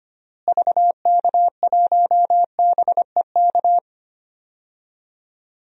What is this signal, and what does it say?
Morse code, heard as a keyed tone, about 25 wpm, 700 Hz. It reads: VK1BEK
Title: DXCC-100 Call Signs - 6 Characters - Repeated 3x - 25wpm